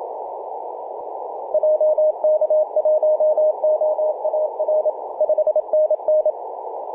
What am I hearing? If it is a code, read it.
Morse code, heard as a keyed tone, about 27 wpm, 620 Hz, with radio noise. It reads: JK1OAR 5NN
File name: 3d2se-18-080mhz-cw